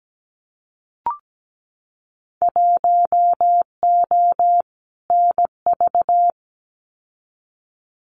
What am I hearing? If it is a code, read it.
Morse code, heard as a keyed tone, about 17 wpm, 700 Hz. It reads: 1O NV
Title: Winter Field Day Exchanges Encore 17wpm